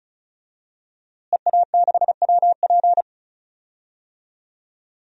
Morse code, heard as a keyed tone, about 36 wpm, 700 Hz. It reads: EA6WP
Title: DXCC-100 Call Signs - 5 Characters - Repeated 3x - 35wpm